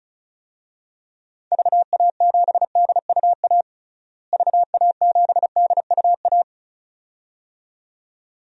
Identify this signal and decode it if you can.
Morse code, heard as a keyed tone, about 36 wpm, 700 Hz. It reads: VA7DUA VA7DUA
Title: DXCC-100 Call Signs - 6 Characters - Repeated 2x - 35wpm